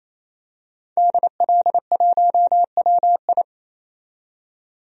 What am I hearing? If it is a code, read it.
Morse code, heard as a keyed tone, about 28 wpm, 700 Hz. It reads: DL1WI
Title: DXCC-100 Call Signs - 5 Characters - 28wpm